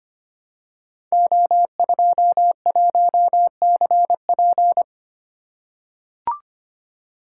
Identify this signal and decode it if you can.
Morse code, heard as a keyed tone, about 25 wpm, 700 Hz. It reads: O21CP
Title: Code Groups - 5 Letter:Numbers - 25wpm